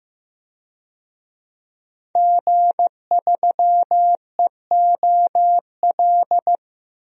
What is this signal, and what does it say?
Morse code, heard as a keyed tone, about 15 wpm, 700 Hz. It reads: G3EOL
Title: DXCC-100 Call Signs - 5 Characters - Repeated 3x - 15wpm